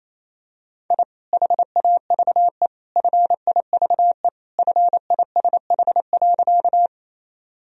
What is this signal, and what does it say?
Morse code, heard as a keyed tone, about 28 wpm, 700 Hz. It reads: I HAVE FIVE FISH.
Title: Sentences from Top 1000 Words - Review Part 2 - 28wpm